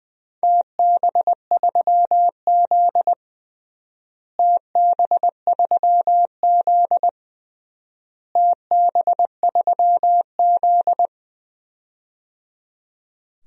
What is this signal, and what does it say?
Morse code, heard as a keyed tone, about 21 wpm, 700 Hz. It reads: TB3Z TB3Z TB3Z
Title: DXCC-100 Call Signs - 4 Characters - Repeated 3x - 20wpm